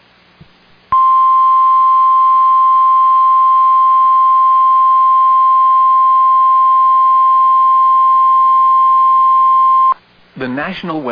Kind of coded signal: Emergency Alert System (SAME) tones